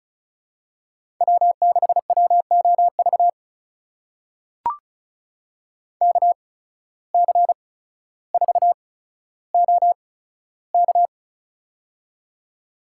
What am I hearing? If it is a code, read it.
Morse code, heard as a keyed tone, about 36 wpm, spaced as for 15 wpm, 700 Hz. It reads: W6WOV KC4OK
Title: USA Call Signs - 5 Characters - 8x Character Spacing - 35wpm